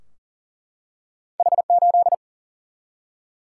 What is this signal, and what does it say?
Morse code, heard as a keyed tone, about 38 wpm, 700 Hz. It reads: H8
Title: Single-Letter Single-Number 40wpm